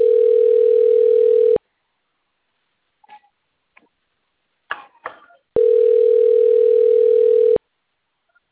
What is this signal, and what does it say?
Ringback tone (North America), 440 + 480 Hz.